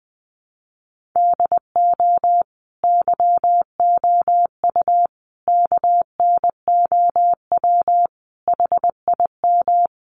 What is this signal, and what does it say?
Morse code, heard as a keyed tone, about 20 wpm, 700 Hz. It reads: DO YOU KNOW HIM
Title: Sentences from Top 1400 Words 20wpm